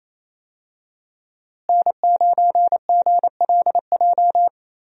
Morse code, heard as a keyed tone, about 28 wpm, 700 Hz. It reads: N9GLJ
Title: USA Call Signs - 5 Characters - Repeated 3x - 28wpm